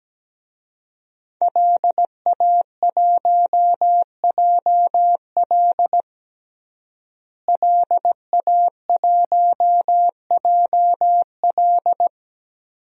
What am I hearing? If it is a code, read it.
Morse code, heard as a keyed tone, about 17 wpm, 700 Hz. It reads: LA1JL LA1JL
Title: DXCC-100 Call Signs - 5 Characters - Repeated 2x - 17wpm